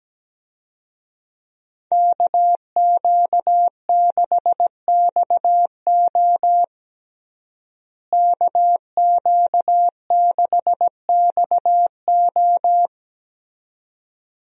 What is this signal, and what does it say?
Morse code, heard as a keyed tone, about 17 wpm, 700 Hz. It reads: KQ6XO KQ6XO
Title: USA Call Signs - 5 Characters - Repeated 2x - 17wpm